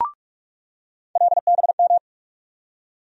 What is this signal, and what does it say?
Morse code, heard as a keyed tone, about 41 wpm, 700 Hz. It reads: LBM
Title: USA Call Sign Postfixes - 3 Letters 45wpm